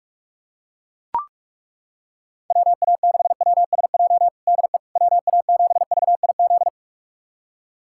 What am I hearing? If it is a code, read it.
Morse code, heard as a keyed tone, about 41 wpm, 700 Hz. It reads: WA6WSJ DE WA7VIZ
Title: USA Callsign DE Callsign - 6 Characters - 45wpm